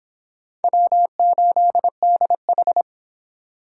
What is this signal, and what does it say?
Morse code, heard as a keyed tone, about 25 wpm, 700 Hz. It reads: W8DH